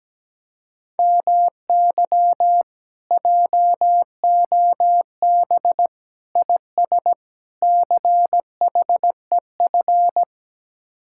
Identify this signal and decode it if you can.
Morse code, heard as a keyed tone, about 17 wpm, 700 Hz. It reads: MY JOB IS CHEF
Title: Speed-Racing Common QSO Phrases - 17wpm